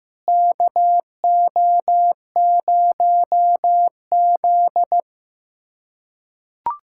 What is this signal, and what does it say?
Morse code, heard as a keyed tone, about 15 wpm, 700 Hz. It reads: KO0Z
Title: K1USN SST QSO Practice - 15wpm